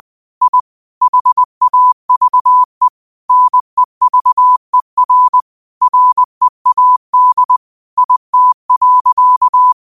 Morse code, heard as a keyed tone, about 20 wpm, 1000 Hz. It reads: I HAVE NEVER READ IT.
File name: qotdfiles / 20wpm